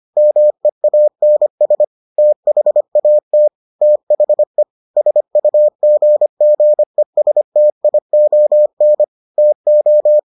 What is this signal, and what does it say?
Morse code, heard as a keyed tone, about 25 wpm, 600 Hz. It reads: MEANS THAT THE SUGGESTION TO